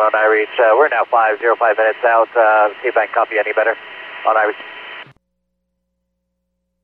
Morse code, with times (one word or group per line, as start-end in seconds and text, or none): none